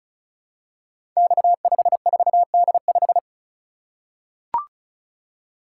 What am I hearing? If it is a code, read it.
Morse code, heard as a keyed tone, about 36 wpm, 700 Hz. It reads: X54D5
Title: Code Groups - 5 Letter:Numbers - 35wpm